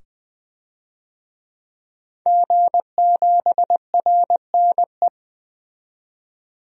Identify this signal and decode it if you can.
Morse code, heard as a keyed tone, about 20 wpm, 700 Hz. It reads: G7RNE
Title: DXCC-100 Call Signs - 5 Characters - 20wpm